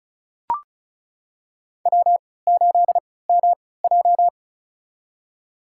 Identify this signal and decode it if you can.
Morse code, heard as a keyed tone, about 34 wpm, spaced as for 20 wpm, 700 Hz. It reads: W8MJ
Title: USA Call Signs - 4 Characters - 3x Character Spacing - 35wpm